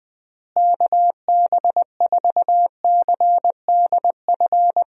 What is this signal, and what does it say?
Morse code, heard as a keyed tone, about 20 wpm, 700 Hz. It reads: KB4CDF
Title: USA Call Signs - 6 Characters - Repeated 2x - 20wpm